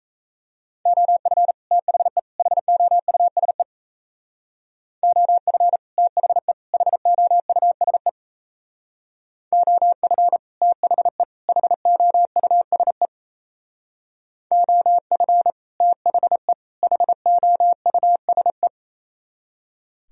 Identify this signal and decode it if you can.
Morse code, heard as a keyed tone, about 36 wpm, 700 Hz. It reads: OF THE HOUSE OF THE HOUSE OF THE HOUSE OF THE HOUSE